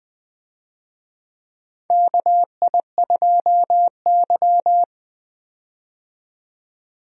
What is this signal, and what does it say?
Morse code, heard as a keyed tone, about 20 wpm, 700 Hz. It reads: KI2Y
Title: USA Call Signs - 4 Characters - 20wpm